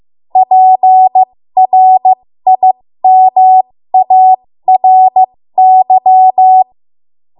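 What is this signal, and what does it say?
Morse code, heard as a keyed tone, about 15 wpm, spaced as for 13 wpm, 750 Hz. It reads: PRIMARY
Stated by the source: 13 wpm Morse code practice archive transmission.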